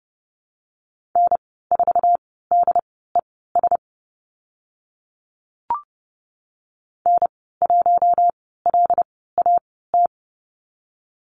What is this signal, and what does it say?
Morse code, heard as a keyed tone, about 30 wpm, spaced as for 17 wpm, 700 Hz. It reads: N4DES N1LAT
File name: Lesson 14g - Number 2 - Call Signs 30wpm